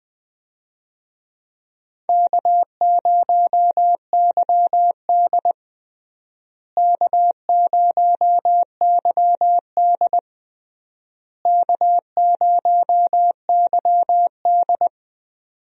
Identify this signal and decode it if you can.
Morse code, heard as a keyed tone, about 20 wpm, 700 Hz. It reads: K0YD K0YD K0YD
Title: USA Call Signs - 4 Characters - Repeated 3x - 20wpm